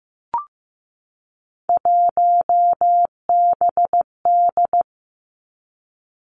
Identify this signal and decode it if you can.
Morse code, heard as a keyed tone, about 15 wpm, 700 Hz. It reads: 1BD
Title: Three Challenging Letter-Number Combinations 15wpm